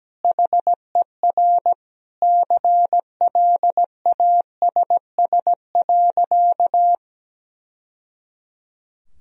Morse code, heard as a keyed tone, about 17 wpm, 700 Hz. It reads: HER CLASS.
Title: Sentences from Top 1000 Words - Review Part 2 - 17wpm